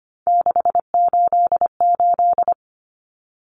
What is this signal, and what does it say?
Morse code, heard as a keyed tone, about 25 wpm, 700 Hz. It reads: 688
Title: Numbers - 3-Digit 25wpm